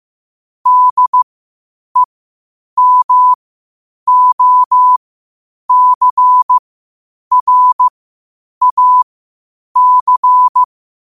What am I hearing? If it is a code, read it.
Morse code, heard as a keyed tone, about 15 wpm, spaced as for 8 wpm, 1000 Hz. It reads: DEMOCRAC
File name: qotdfiles / 8wpm